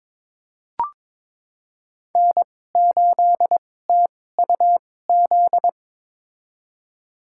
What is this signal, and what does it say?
Morse code, heard as a keyed tone, about 22 wpm, spaced as for 16 wpm, 700 Hz. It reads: N8TUZ
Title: USA Call Signs - 5 Characters - 2x Character Spacing - 22wpm